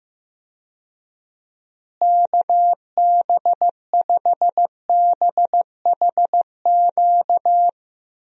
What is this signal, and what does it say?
Morse code, heard as a keyed tone, about 16 wpm, 700 Hz. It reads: KB5BHQ
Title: USA Call Signs - 6 Characters - Repeated 3x - 15wpm